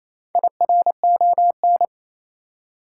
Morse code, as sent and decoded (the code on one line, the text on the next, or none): .. .-. --- -.
IRON